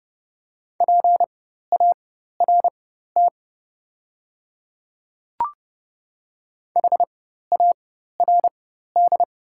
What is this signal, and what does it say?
Morse code, heard as a keyed tone, about 30 wpm, spaced as for 14 wpm, 700 Hz. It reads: PART HARD
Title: 4x Character Spacing - Top 300 Words - 30wpm